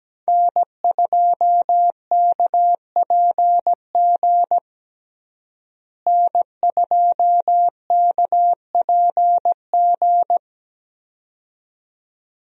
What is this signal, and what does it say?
Morse code, heard as a keyed tone, about 17 wpm, 700 Hz. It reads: N2KPG N2KPG
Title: USA Call Signs - 5 Characters - Repeated 2x - 17wpm